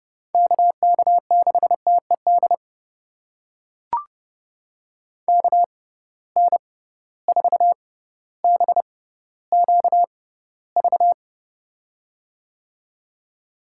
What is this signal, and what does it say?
Morse code, heard as a keyed tone, about 29 wpm, spaced as for 15 wpm, 700 Hz. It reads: KK6TED KN4BQV
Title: USA Call Signs - 6 Characters - 6x Character Spacing - 30wpm